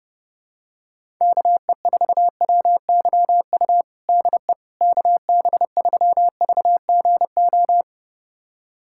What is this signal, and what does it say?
Morse code, heard as a keyed tone, about 30 wpm, 700 Hz. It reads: KE4WYU DE KB3VGO
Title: USA Callsign DE Callsign - 6 Characters - 30wpm